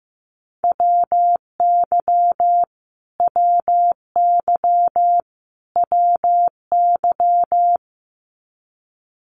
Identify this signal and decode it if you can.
Morse code, heard as a keyed tone, about 15 wpm, 700 Hz. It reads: WY WY WY
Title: US States Repeated 3x 15wpm